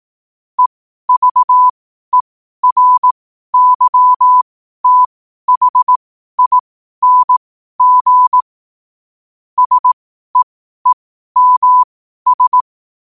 Morse code, heard as a keyed tone, about 18 wpm, spaced as for 12 wpm, 1000 Hz. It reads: EVERYTHING SEEMS
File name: qotdfiles / 12wpm